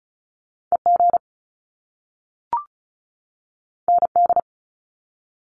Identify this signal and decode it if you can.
Morse code, heard as a keyed tone, about 36 wpm, 700 Hz. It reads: EG ND
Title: Top Two Letter Combinations in Words 35wpm